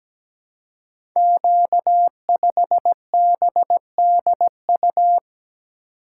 Morse code, heard as a keyed tone, about 17 wpm, 700 Hz. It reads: Q5BDU
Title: Code Groups - 5 Letter:Numbers - 17wpm